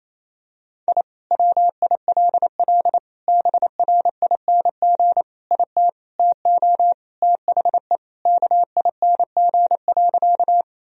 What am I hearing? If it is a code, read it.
Morse code, heard as a keyed tone, about 28 wpm, 700 Hz. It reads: I WILL BRING IT TO THE KING.
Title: Sentences from Top 1000 Words - Review Part 3 - 28wpm